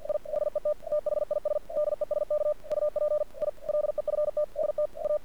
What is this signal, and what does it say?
Morse code, heard as a keyed tone, about 49 wpm, 600 Hz. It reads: I HET THIS DESK UP I LEFT ST A
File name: archived / CRSnet